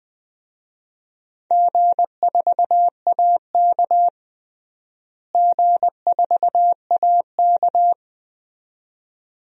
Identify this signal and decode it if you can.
Morse code, heard as a keyed tone, about 20 wpm, 700 Hz. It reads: G4AK G4AK
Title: DXCC-100 Call Signs - 4 Characters - Repeated 2x - 20wpm